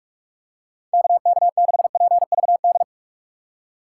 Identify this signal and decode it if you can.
Morse code, heard as a keyed tone, about 46 wpm, 700 Hz. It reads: KK6PVD
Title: USA Call Signs - 6x Character Spacing - 45wpm